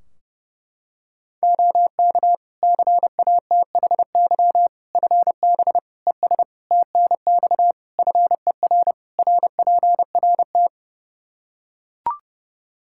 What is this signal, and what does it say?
Morse code, heard as a keyed tone, about 30 wpm, 700 Hz. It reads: OK CATHY FB ES TNX FER RPRT
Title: Practice QSO - Exchange 2 - Info - 30wpm